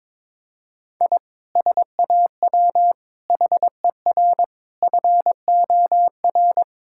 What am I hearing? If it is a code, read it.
Morse code, heard as a keyed tone, about 22 wpm, 700 Hz. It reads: I SAW HER FOR
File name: practice / 22wpm / Sentences with 4-Word Phrases - 22wpm